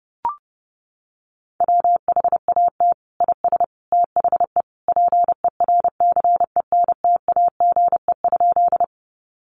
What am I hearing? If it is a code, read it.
Morse code, heard as a keyed tone, about 30 wpm, 700 Hz. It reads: WHAT IS THE PERCENTAGE?
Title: Sentences from Top 1300 Words 30wpm